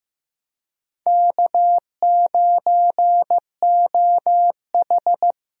Morse code, heard as a keyed tone, about 15 wpm, 700 Hz. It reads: K9OH
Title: USA Call Signs - 4 Characters - 15wpm